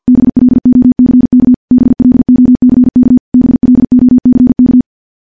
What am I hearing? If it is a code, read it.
Morse code, heard as a keyed tone, about 50 wpm, 250 Hz. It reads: BLOCK BLOCK BLOCK
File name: morse / morse-training-50WPM